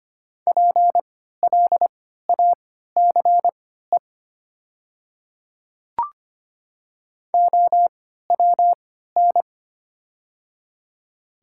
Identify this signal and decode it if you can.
Morse code, heard as a keyed tone, about 25 wpm, spaced as for 14 wpm, 700 Hz. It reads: PLACE OWN
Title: 3x Character Spacing - Top 300 Words - 25wpm